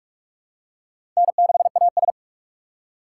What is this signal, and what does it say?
Morse code, heard as a keyed tone, about 42 wpm, 700 Hz. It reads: N6AS
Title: USA Call Signs - 4 Characters - 2x Character Spacing - 45wpm